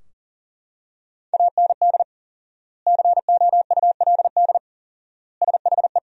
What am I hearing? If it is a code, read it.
Morse code, heard as a keyed tone, about 40 wpm, 700 Hz. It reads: AND COULD SHE
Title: 4x Word-Spacing - Sentences from Top 100 Words - 40wpm